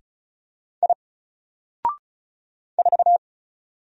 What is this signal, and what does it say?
Morse code, heard as a keyed tone, about 34 wpm, 700 Hz. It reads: I 4
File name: Single-Letter-Number-35wpm